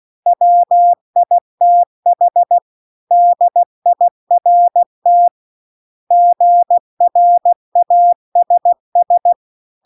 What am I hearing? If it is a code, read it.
Morse code, heard as a keyed tone, about 16 wpm, 700 Hz. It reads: WITH DIRT GRASS
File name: CW-The-Big-Twoheaded-River-2-16WPM